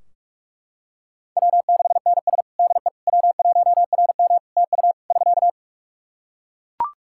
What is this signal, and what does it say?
Morse code, heard as a keyed tone, about 44 wpm, 700 Hz. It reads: W6NS DE W1RM TU <SK>
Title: Practice QSO - Exchange 4 - Ending - 45wpm